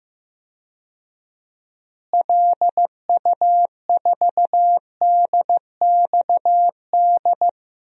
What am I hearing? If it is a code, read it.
Morse code, heard as a keyed tone, about 16 wpm, 700 Hz. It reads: LU4DXD
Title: DXCC-100 Call Signs - 6 Characters - Repeated 3x - 15wpm